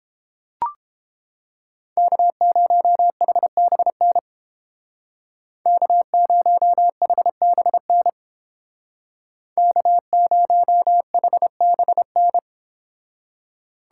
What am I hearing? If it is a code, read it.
Morse code, heard as a keyed tone, about 31 wpm, 700 Hz. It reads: K0HBN K0HBN K0HBN